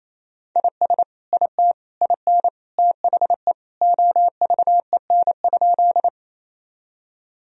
Morse code, heard as a keyed tone, about 28 wpm, 700 Hz. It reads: IS IT IN THE OVEN?
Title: Sentences from Top 1000 Words - Review Part 4 - 28wpm